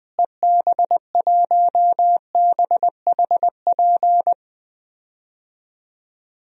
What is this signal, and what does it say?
Morse code, heard as a keyed tone, about 20 wpm, 700 Hz. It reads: EB1BHP
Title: DXCC-100 Call Signs - 6 Characters - Repeated 2x - 20wpm